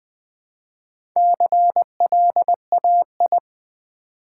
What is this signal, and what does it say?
Morse code, heard as a keyed tone, about 20 wpm, 700 Hz. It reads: CLAI
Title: Build A Word - 1 Letter At A Time - 5-Letter Words - 20wpm